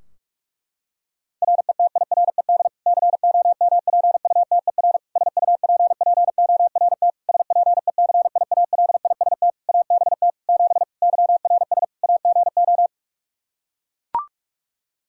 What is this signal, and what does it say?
Morse code, heard as a keyed tone, about 41 wpm, 700 Hz. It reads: RETIRED COMPUTER SUPPORT SPECIALIST ABT 7 YRS AGO
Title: Practice QSO - Exchange 3 - More Info - 45wpm